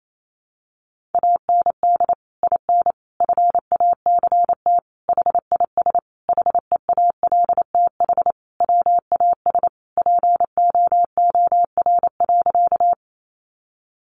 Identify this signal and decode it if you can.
Morse code, heard as a keyed tone, about 28 wpm, 700 Hz. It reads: AND IN FACT HIS HEALTH WAS POOR.
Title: Sentences from Top 1000 Words 28wpm